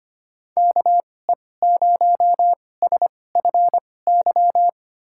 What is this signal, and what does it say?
Morse code, heard as a keyed tone, about 25 wpm, spaced as for 18 wpm, 700 Hz. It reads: KE0SFY
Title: USA Call Signs - 6 Characters - 2x Character Spacing - 25wpm